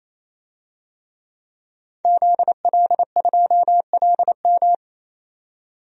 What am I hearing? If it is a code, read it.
Morse code, heard as a keyed tone, about 28 wpm, 700 Hz. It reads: ZL2LM
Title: DXCC-100 Call Signs - 5 Characters - 28wpm